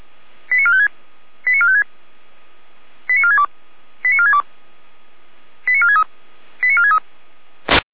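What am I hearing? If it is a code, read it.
Five-tone selective call (ZVEI), heard as quick runs of five tones, tones of about 70 ms. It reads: 89456 89456 89462 89462 89463 89463
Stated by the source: ZVEI five-tone selective call received via SDR.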